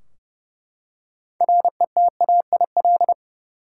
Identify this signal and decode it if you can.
Morse code, heard as a keyed tone, about 29 wpm, 700 Hz. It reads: RETAIL
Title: Practice QSO - Exchange 3 - More Info - 30wpm